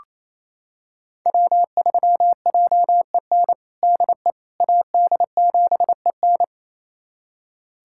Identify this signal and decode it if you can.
Morse code, heard as a keyed tone, about 28 wpm, 700 Hz. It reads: W3JEN DE AD7EN
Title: USA Callsign DE Callsign - 5 Characters - 28wpm